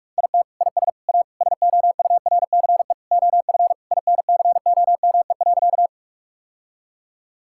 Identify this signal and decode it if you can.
Morse code, heard as a keyed tone, about 46 wpm, 700 Hz. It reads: IT IS A SOURCE OF INCOME.
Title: Sentences from Top 1000 Words - Review Part 5 - 45wpm